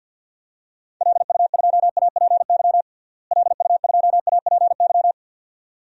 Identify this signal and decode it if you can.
Morse code, heard as a keyed tone, about 52 wpm, 700 Hz. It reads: LU2APY LU2APY
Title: DXCC-100 Call Signs - 6 Characters - Repeated 2x - 50wpm